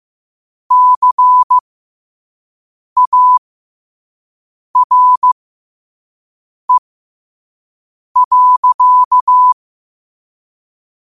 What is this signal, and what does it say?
Morse code, heard as a keyed tone, about 15 wpm, spaced as for 5 wpm, 1000 Hz. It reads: CARE.
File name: qotdfiles / 5wpm